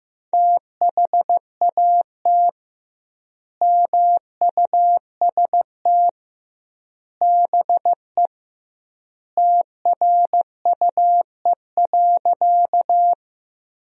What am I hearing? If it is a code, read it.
Morse code, heard as a keyed tone, about 15 wpm, 700 Hz. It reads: THAT MUST BE TRUE.
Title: 2x Word-Spacing - Sentences from Top 400 Words - 15wpm